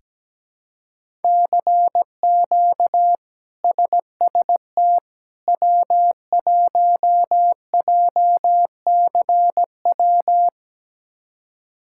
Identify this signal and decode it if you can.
Morse code, heard as a keyed tone, about 17 wpm, 700 Hz. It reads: CQ SST W1JCW
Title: K1USN SST QSO Practice - 17wpm